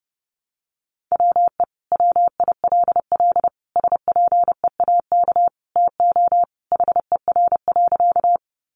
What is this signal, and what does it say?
Morse code, heard as a keyed tone, about 30 wpm, 700 Hz. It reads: WE WILL SPEAK TO HER.
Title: Sentences from Top 1100 Words 30wpm